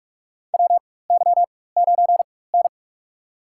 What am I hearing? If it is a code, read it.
Morse code, heard as a keyed tone, about 46 wpm, spaced as for 21 wpm, 700 Hz. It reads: WY9N
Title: USA Call Signs - 4 Characters - 4x Character Spacing - 45wpm